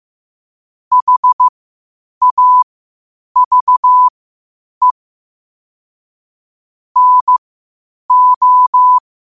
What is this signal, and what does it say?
Morse code, heard as a keyed tone, about 15 wpm, spaced as for 8 wpm, 1000 Hz. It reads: HAVE NO
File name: qotdfiles / 8wpm